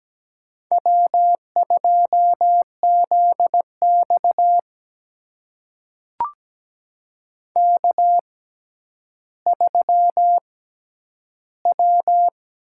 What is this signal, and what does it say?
Morse code, heard as a keyed tone, about 17 wpm, spaced as for 10 wpm, 700 Hz. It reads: W2ZX K3W
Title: USA Call Signs - 4 Characters - 6x Character Spacing - 17wpm